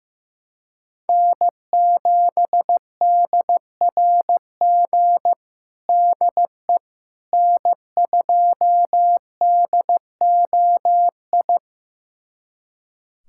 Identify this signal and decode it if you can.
Morse code, heard as a keyed tone, about 15 wpm, 700 Hz. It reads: N7DRG DE N2DOI
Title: USA Callsign DE Callsign - 5 Characters - 15wpm